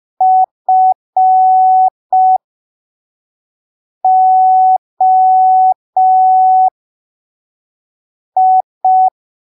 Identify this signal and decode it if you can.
Morse code, heard as a keyed tone, about 5 wpm, spaced as for 3 wpm, 750 Hz. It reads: FOI
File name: arquivos / texto / texto 14 05ppm